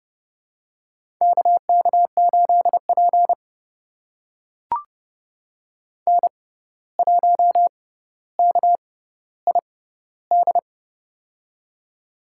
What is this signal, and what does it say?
Morse code, heard as a keyed tone, about 30 wpm, spaced as for 14 wpm, 700 Hz. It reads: KK8P N1KID
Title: USA Call Signs - 6x Character Spacing - 30wpm